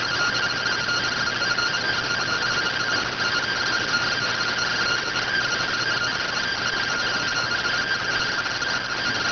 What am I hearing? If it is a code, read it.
Two-tone teleprinter signal, about 170 Hz shift.